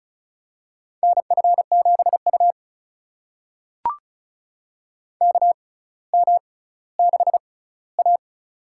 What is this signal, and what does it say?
Morse code, heard as a keyed tone, about 36 wpm, spaced as for 18 wpm, 700 Hz. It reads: NF7U KM6A
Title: USA Call Signs - 4 Characters - 6x Character Spacing - 35wpm